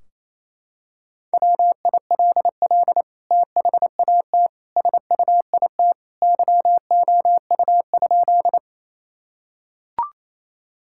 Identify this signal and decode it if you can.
Morse code, heard as a keyed tone, about 28 wpm, 700 Hz. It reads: WILL THAT SUIT YOU?
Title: Sentences from Top 1600 Words 28wpm